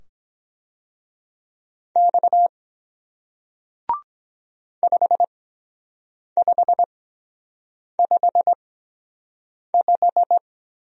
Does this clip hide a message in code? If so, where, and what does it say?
Yes: Morse code from 2.0 to 10.4 s, heard as a keyed tone, about 23 wpm, spaced as for 7 wpm, 700 Hz. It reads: X 5555